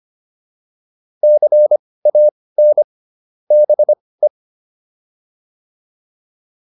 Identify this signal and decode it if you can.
Morse code, heard as a keyed tone, about 25 wpm, spaced as for 18 wpm, 600 Hz. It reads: CAN BE